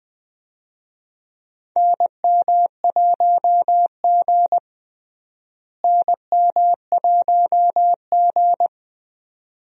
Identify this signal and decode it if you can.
Morse code, heard as a keyed tone, about 20 wpm, 700 Hz. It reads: NM1G NM1G
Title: USA Call Signs - 4 Characters - Repeated 2x - 20wpm